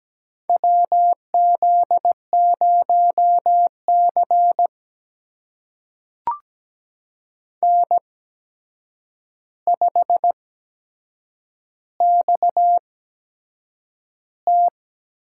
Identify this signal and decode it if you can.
Morse code, heard as a keyed tone, about 17 wpm, spaced as for 7 wpm, 700 Hz. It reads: WZ0C N5XT
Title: USA Call Signs - 4 Characters - 8x Character Spacing - 17wpm